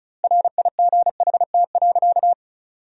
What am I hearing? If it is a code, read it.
Morse code, heard as a keyed tone, about 36 wpm, 700 Hz. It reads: RIGHT.
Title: 2x Word-Spacing - Sentences from Top 300 Words - 35wpm